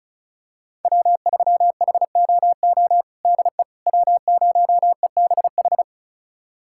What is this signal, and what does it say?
Morse code, heard as a keyed tone, about 36 wpm, 700 Hz. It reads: W3HOO DE W0EBH
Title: USA Callsign DE Callsign - 5 Characters - 35wpm